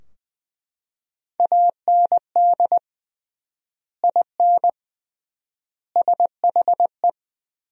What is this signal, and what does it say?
Morse code, heard as a keyed tone, about 21 wpm, 700 Hz. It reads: AND IN SHE
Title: 3x Word-Spacing - Sentences from Top 200 Words - 20wpm